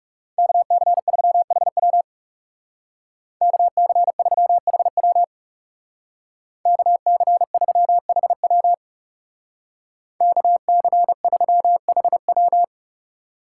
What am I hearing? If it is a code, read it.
Morse code, heard as a keyed tone, about 39 wpm, 700 Hz. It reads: KC3HW KC3HW KC3HW KC3HW